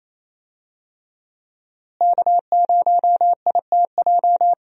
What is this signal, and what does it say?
Morse code, heard as a keyed tone, about 27 wpm, 700 Hz. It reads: K0ITJ
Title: USA Call Signs - 5 Characters - Repeated 3x - 28wpm